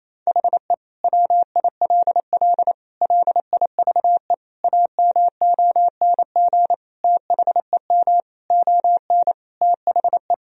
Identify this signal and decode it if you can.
Morse code, heard as a keyed tone, about 27 wpm, 700 Hz. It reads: HE WILL LIVE AMONG THEM ON THE
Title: Sentences from Top 1000 Words - Review Part 3 - 28wpm